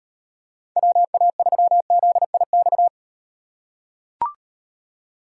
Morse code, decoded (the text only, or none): WA3ZIX